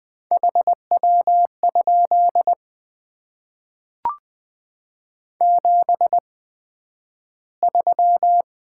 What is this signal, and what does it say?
Morse code, heard as a keyed tone, about 20 wpm, spaced as for 11 wpm, 700 Hz. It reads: HW? 73
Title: Top 90 QSO Elements - 8x Character Spacing - 20wpm